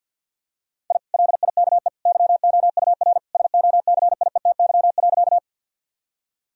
Morse code, heard as a keyed tone, about 50 wpm, 700 Hz. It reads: I LIKE YOUR SOCIETY.
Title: Sentences from Top 1000 Words - Review Part 4 - 50wpm